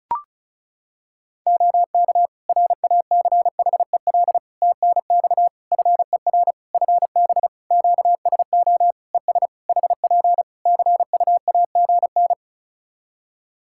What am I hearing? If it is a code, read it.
Morse code, heard as a keyed tone, about 36 wpm, 700 Hz. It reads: OK RACHEL TNX FER FB QSO ES HP CUAGN
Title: K1USN SST QSO Practice 35wpm